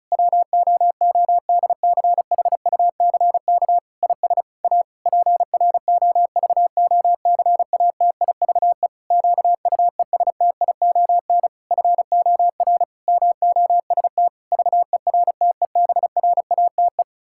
Morse code, heard as a keyed tone, about 36 wpm, 700 Hz. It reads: WOODCHUCK IS A PROVOCATIVE QUESTION FOR MOST VERTEBRATE
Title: Pangram Challenge v2 - 35wpm